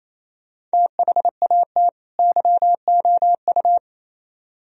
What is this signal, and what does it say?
Morse code, heard as a keyed tone, about 34 wpm, 700 Hz. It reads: THAT YOU
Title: Speed-Racing - Sets of 2 Words 28wpm